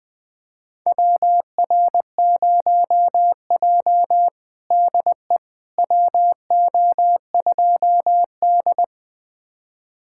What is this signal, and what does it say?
Morse code, heard as a keyed tone, about 20 wpm, 700 Hz. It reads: WR0J DE WO2D
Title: USA Callsign DE Callsign - 4 Characters - 20wpm